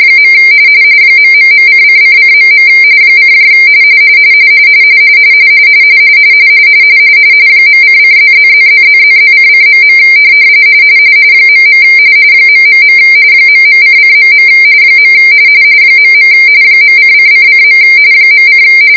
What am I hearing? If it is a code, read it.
RTTY, about 45.45 baud. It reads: . ...::::.. : .::+++++++::.. .. ... ..::.. .. .::::. .: